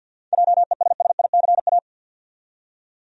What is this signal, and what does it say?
Morse code, heard as a keyed tone, about 47 wpm, 700 Hz. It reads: JESSICA
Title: USA Common Names - 50wpm